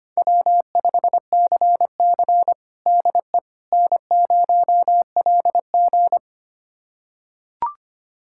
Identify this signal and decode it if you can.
Morse code, heard as a keyed tone, about 25 wpm, 700 Hz. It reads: W5CC DE N0LG
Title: USA Callsign DE Callsign - 4 Characters - 25wpm